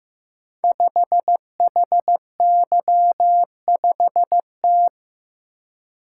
Morse code, heard as a keyed tone, about 15 wpm, 700 Hz. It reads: 5HY5T
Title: Code Groups - 5 Letter:Numbers - 15wpm